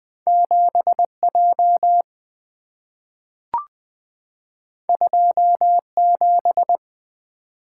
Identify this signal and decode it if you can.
Morse code, heard as a keyed tone, about 20 wpm, 700 Hz. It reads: 7J 27
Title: Two Letter-Number Combinations - 20wpm